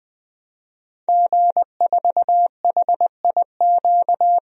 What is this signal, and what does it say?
Morse code, heard as a keyed tone, about 20 wpm, 700 Hz. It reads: G4HIQ
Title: DXCC-100 Call Signs - 5 Characters - 20wpm